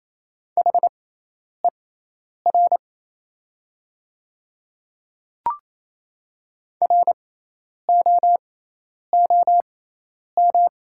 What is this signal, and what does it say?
Morse code, heard as a keyed tone, about 28 wpm, spaced as for 10 wpm, 700 Hz. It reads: HER ROOM
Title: 6x Character Spacing - Top 300 Words - 28wpm